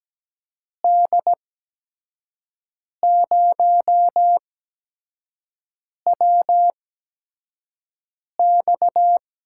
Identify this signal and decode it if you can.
Morse code, heard as a keyed tone, about 17 wpm, spaced as for 5 wpm, 700 Hz. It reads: D0WX
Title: USA Call Signs - 6 Characters - 8x Character Spacing - 17wpm